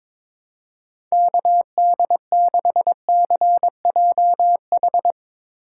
Morse code, heard as a keyed tone, about 21 wpm, 700 Hz. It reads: KD6CJH
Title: USA Call Signs - 6 Characters - 4x Character Spacing - 22wpm